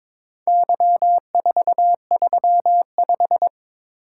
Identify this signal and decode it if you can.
Morse code, heard as a keyed tone, about 22 wpm, 700 Hz. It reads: Y435
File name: practice / 22wpm / Four Letter-Number Combinations - 22wpm